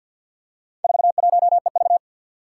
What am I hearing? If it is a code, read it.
Morse code, heard as a keyed tone, about 49 wpm, 700 Hz. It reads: 41EV
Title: Four Challenging Letter-Number Combinations 50wpm